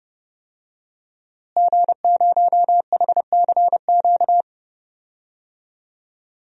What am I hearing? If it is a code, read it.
Morse code, heard as a keyed tone, about 30 wpm, 700 Hz. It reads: G0HCQ
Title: DXCC-100 Call Signs - 5 Characters - Repeated 3x - 30wpm